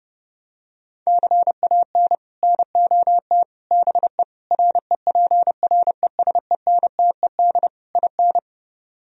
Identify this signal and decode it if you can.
Morse code, heard as a keyed tone, about 32 wpm, 700 Hz. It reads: CAN NOT BE REPRESENTED IN